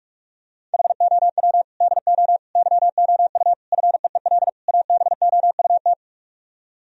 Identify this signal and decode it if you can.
Morse code, heard as a keyed tone, about 44 wpm, 700 Hz. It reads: HOW DO YOU FEEL ABOUT